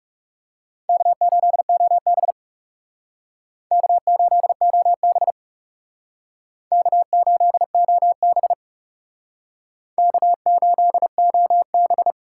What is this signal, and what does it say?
Morse code, heard as a keyed tone, about 37 wpm, 700 Hz. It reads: K8OB K8OB K8OB K8OB
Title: Speed Racing - USA 1x2 Call Signs 30wpm